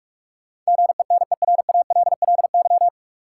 Morse code, heard as a keyed tone, about 46 wpm, 700 Hz. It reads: GENERALLY
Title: Sentences from Top 600 Words - New Word Drill - 45wpm